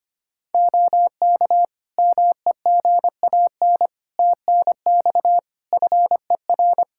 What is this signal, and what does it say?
Morse code, heard as a keyed tone, about 25 wpm, 700 Hz. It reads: OK MEGAN TNX FER
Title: Practice QSO - Exchange 4 - Ending - 25wpm